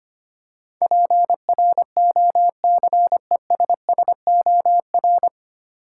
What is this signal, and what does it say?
Morse code, heard as a keyed tone, about 25 wpm, 700 Hz. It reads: PROCESSOR
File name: Nine Letter Words - 25wpm